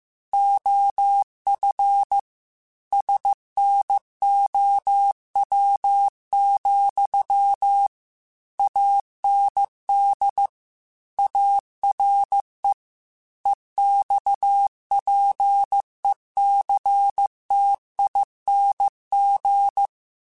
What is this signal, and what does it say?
Morse code, heard as a keyed tone, about 15 wpm, 800 Hz. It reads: OF SNOW, AND ARE EXPECTING
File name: QSOs / QSOs-mp3s / WA7QOX 1@13